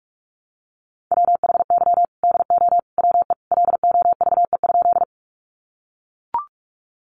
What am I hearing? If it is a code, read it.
Morse code, heard as a keyed tone, about 44 wpm, 700 Hz. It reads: WHY DO WE LOVE?
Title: Sentences from Top 400 Words 45wpm